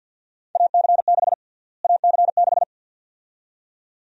Morse code, heard as a keyed tone, about 48 wpm, 700 Hz. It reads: AC6 AC6
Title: USA Call Sign Prefixes - Repeated 2x - 50wpm